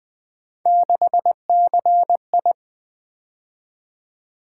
Morse code, heard as a keyed tone, about 20 wpm, 700 Hz. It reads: 6CI
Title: Three Letter-Number Combinations - 20wpm